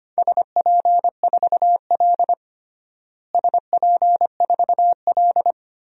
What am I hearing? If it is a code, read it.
Morse code, heard as a keyed tone, about 25 wpm, 700 Hz. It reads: SP4L SP4L
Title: DXCC-100 Call Signs - 4 Characters - Repeated 3x - 25wpm